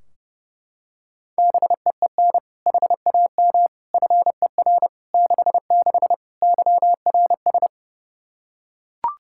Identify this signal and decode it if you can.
Morse code, heard as a keyed tone, about 30 wpm, 700 Hz. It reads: BEEN HAM FER 66 YRS
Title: Practice QSO - Exchange 3 - More Info - 30wpm